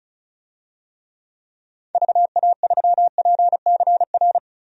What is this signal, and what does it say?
Morse code, heard as a keyed tone, about 34 wpm, 700 Hz. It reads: VA3PCR
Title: DXCC-100 Call Signs - 6 Characters - Repeated 3x - 35wpm